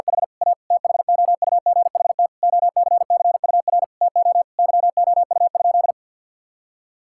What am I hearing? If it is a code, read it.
Morse code, heard as a keyed tone, about 49 wpm, 700 Hz. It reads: H A THOUGHT OCCUR TO YOU?
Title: Sentences from Top 1000 Words - Review Part 6 - 50wpm